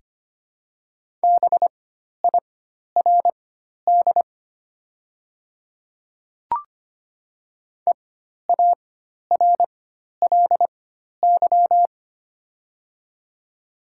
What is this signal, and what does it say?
Morse code, heard as a keyed tone, about 25 wpm, spaced as for 12 wpm, 700 Hz. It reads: BIRD EARLY
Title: 4x Character Spacing - Top 500 Words - 25wpm